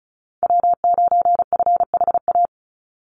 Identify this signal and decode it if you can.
Morse code, heard as a keyed tone, about 34 wpm, 700 Hz. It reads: W9FHA
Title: NA SOTA Chaser Callsigns 35wpm